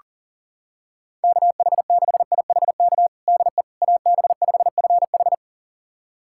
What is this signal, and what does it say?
Morse code, heard as a keyed tone, about 37 wpm, 700 Hz. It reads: KH6IHK DE AB5FH
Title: USA Callsign DE Callsign 40wpm